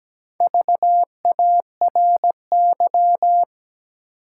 Morse code, heard as a keyed tone, about 17 wpm, 700 Hz. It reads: VARY
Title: Sentences from Top 1200 Words - New Word Drill - 17wpm